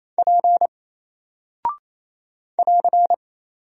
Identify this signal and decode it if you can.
Morse code, heard as a keyed tone, about 27 wpm, 700 Hz. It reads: P <AR>
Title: Single Character (Letter:Number:Punctuation:Prosign) 28wpm